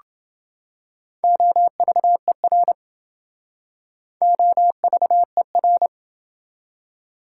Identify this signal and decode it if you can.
Morse code, heard as a keyed tone, about 27 wpm, 700 Hz. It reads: OVER OVER